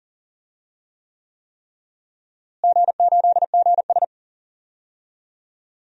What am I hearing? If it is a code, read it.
Morse code, heard as a keyed tone, about 39 wpm, 700 Hz. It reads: G8GS
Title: DXCC-100 Call Signs - 4 Characters - Repeated 3x - 40wpm